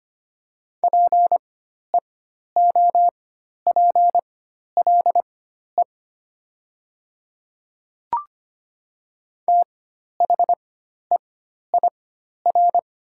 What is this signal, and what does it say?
Morse code, heard as a keyed tone, about 25 wpm, spaced as for 12 wpm, 700 Hz. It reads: PEOPLE THEIR
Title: 4x Character Spacing - Top 200 Words - 25wpm